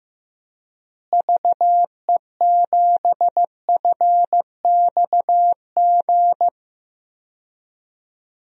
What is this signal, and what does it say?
Morse code, heard as a keyed tone, about 15 wpm, 700 Hz. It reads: VE7FXG
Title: DXCC-100 Call Signs - 6 Characters - Repeated 2x - 15wpm